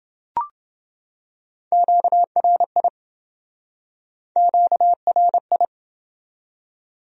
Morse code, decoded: QRI QRI

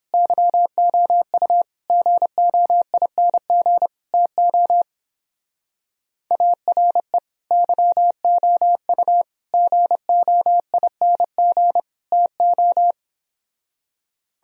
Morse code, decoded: YOU GOING TO ARE YOU GOING TO

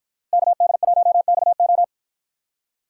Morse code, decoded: KD1XO

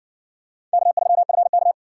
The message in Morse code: K3FK

